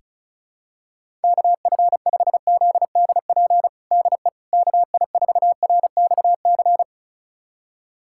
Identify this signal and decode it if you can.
Morse code, heard as a keyed tone, about 35 wpm, 700 Hz. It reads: KF5ZDP DE KI4RXC
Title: USA Callsign DE Callsign 35wpm